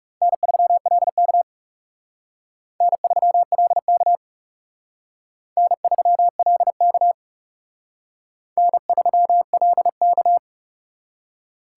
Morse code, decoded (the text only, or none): N3LK N3LK N3LK N3LK